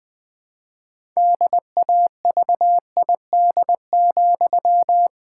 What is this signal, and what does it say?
Morse code, heard as a keyed tone, about 20 wpm, 700 Hz. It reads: DAVID,
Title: Pangram Challenge v2 - 20wpm